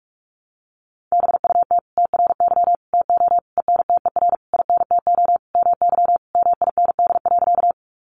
Morse code, heard as a keyed tone, about 44 wpm, 700 Hz. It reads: BUT TRY TO ENTER INTO MY MIND.
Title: Sentences from Top 1300 Words 45wpm